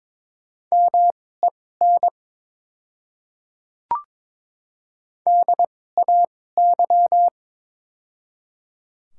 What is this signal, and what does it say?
Morse code, heard as a keyed tone, about 22 wpm, spaced as for 16 wpm, 700 Hz. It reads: MEN DAY
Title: 2x Character Spacing - Top 200 Words - 22wpm